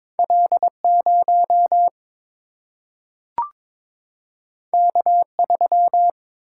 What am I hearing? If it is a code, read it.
Morse code, heard as a keyed tone, about 22 wpm, 700 Hz. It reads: L0 K3
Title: Single-Letter Single-Number 22wpm